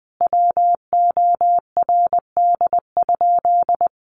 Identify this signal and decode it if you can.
Morse code, heard as a keyed tone, about 20 wpm, 700 Hz. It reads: WORD?
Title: Sentences from Top 1200 Words 20wpm